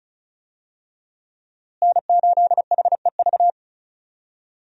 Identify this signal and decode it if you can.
Morse code, heard as a keyed tone, about 34 wpm, 700 Hz. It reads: N8HEV
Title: USA Call Signs - 5 Characters - Repeated 3x - 35wpm